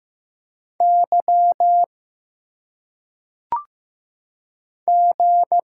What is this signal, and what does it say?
Morse code, heard as a keyed tone, about 15 wpm, 700 Hz. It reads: Y G
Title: Single Letters - Phonetic Pronunciation 15wpm